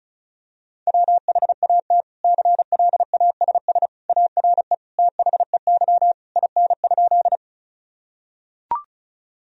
Morse code, heard as a keyed tone, about 34 wpm, 700 Hz. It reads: WHAT CLASS ARE THEY IN?
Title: Sentences from Top 1000 Words - Review Part 2 - 35wpm